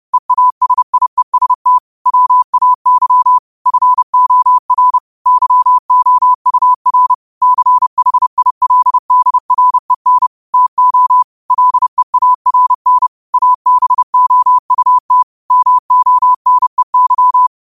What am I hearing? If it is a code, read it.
Morse code, heard as a keyed tone, about 30 wpm, 1000 Hz. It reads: EASIEST WAY FOR YOUR CHILDREN TO LEARN ABOUT MONEY
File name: qotdfiles / 30wpm